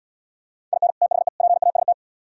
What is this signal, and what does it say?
Morse code, heard as a keyed tone, about 76 wpm, 700 Hz. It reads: IN THE PAST